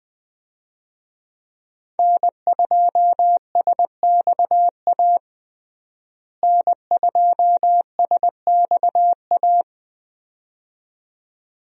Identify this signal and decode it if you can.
Morse code, heard as a keyed tone, about 20 wpm, 700 Hz. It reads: N2SXA N2SXA
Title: USA Call Signs - 5 Characters - Repeated 2x - 20wpm